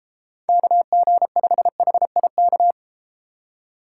Morse code, heard as a keyed tone, about 33 wpm, 700 Hz. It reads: KG5HIK